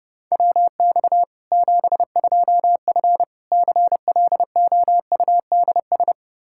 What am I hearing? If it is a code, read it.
Morse code, heard as a keyed tone, about 32 wpm, 700 Hz. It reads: WX 72F CLOUDS
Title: K1USN SST QSO Practice 30wpm